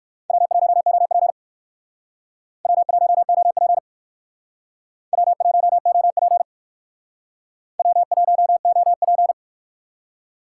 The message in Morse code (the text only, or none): W1OP W1OP W1OP W1OP